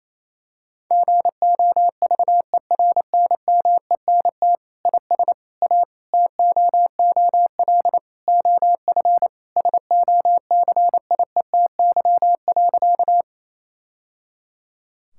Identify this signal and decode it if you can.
Morse code, heard as a keyed tone, about 28 wpm, 700 Hz. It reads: GOVERNMENT IS A TOOL OF SOCIETY.
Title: Sentences from Top 1000 Words - Review Part 6 - 28wpm